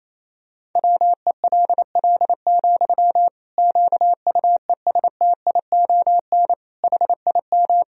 Morse code, heard as a keyed tone, about 28 wpm, 700 Hz. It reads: WELL, QUESTION HIM
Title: Sentences from Top 1000 Words - Review Part 3 - 28wpm